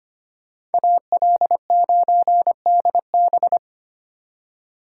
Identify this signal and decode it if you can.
Morse code, heard as a keyed tone, about 31 wpm, 700 Hz. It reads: AL9DB